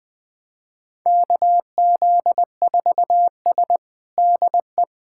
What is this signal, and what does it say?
Morse code, heard as a keyed tone, about 20 wpm, 700 Hz. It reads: KZ4S DE
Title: USA Callsign DE Callsign - 4 Characters - 20wpm